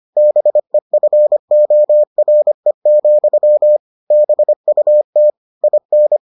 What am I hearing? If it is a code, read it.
Morse code, heard as a keyed tone, about 25 wpm, 600 Hz. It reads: BEFORE, BUT IN